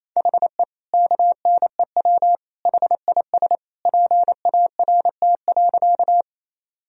Morse code, heard as a keyed tone, about 28 wpm, 700 Hz. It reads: HE KNEW HIS PART.
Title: Sentences from Top 1000 Words - Review Part 1 - 28wpm